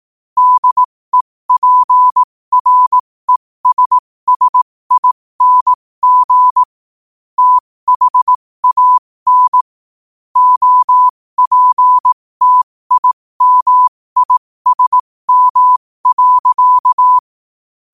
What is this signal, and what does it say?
Morse code, heard as a keyed tone, about 18 wpm, spaced as for 15 wpm, 1000 Hz. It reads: DEPRESSING THAN OPTIMISM.
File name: qotdfiles / 15wpm